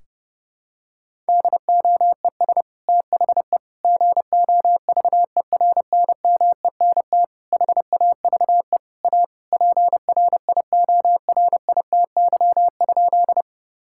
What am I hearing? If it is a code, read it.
Morse code, heard as a keyed tone, about 30 wpm, 700 Hz. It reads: DOES THE GOVERNMENT HAVE A PRIORITY?
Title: Sentences from Top 1600 Words 30wpm